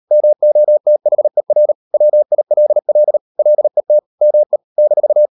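Morse code, heard as a keyed tone, about 38 wpm, 600 Hz. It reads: MOTHER WILL LET ME -
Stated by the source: Morse code practice at 38 WPM, sending Ken Follett's novel Never in English, from CW med Gnister.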